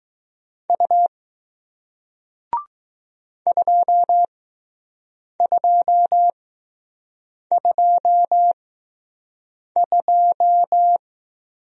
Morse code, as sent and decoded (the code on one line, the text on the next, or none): ..- / ..--- ..--- ..--- ..---
U 2222